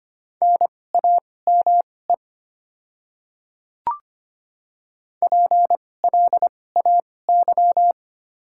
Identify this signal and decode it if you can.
Morse code, heard as a keyed tone, about 25 wpm, spaced as for 18 wpm, 700 Hz. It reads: NAME PLAY
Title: 2x Character Spacing - Top 200 Words - 25wpm